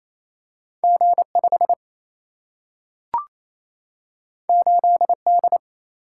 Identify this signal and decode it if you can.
Morse code, heard as a keyed tone, about 28 wpm, 700 Hz. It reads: G 5 8 D
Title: Single-Letter Single-Number 28wpm